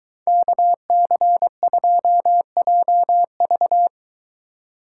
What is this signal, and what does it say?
Morse code, heard as a keyed tone, about 23 wpm, 700 Hz. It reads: KC2JV